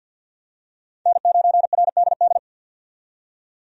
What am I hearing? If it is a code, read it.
Morse code, heard as a keyed tone, about 50 wpm, 700 Hz. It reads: N9RDD
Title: USA Call Signs - 5 Characters - 50wpm